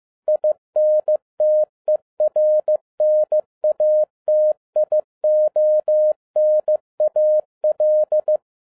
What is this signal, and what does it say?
Morse code, heard as a keyed tone, about 15 wpm, 610 Hz. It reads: INTERNATIONAL
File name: 2011-10-mp3 / n12-JL3YMV 15wpm